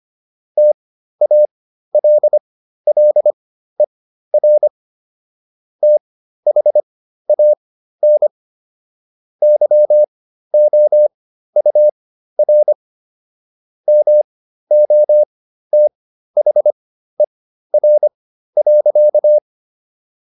Morse code, heard as a keyed tone, about 25 wpm, spaced as for 13 wpm, 600 Hz. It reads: TALLER THAN YOUR MOTHER.